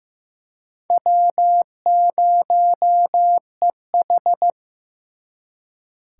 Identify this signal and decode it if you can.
Morse code, heard as a keyed tone, about 18 wpm, 700 Hz. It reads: W0EH
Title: Speed Racing - USA 1x2 Call Signs 15wpm